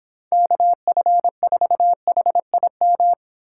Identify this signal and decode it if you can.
Morse code, heard as a keyed tone, about 29 wpm, 700 Hz. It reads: KF4HIM